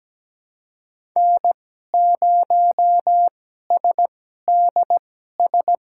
Morse code, heard as a keyed tone, about 17 wpm, spaced as for 12 wpm, 700 Hz. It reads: N0SDS
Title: USA Call Signs - 5 Characters - 2x Character Spacing - 17wpm